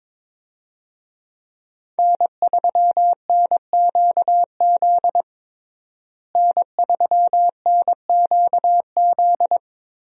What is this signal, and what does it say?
Morse code, heard as a keyed tone, about 23 wpm, 700 Hz. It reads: N3NQZ N3NQZ
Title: USA Call Signs - 5 Characters - Repeated 3x - 22wpm